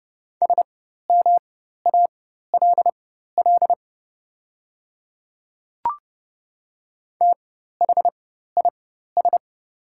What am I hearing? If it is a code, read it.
Morse code, heard as a keyed tone, about 30 wpm, spaced as for 14 wpm, 700 Hz. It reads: SMALL THIS
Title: 4x Character Spacing - Top 200 Words - 30wpm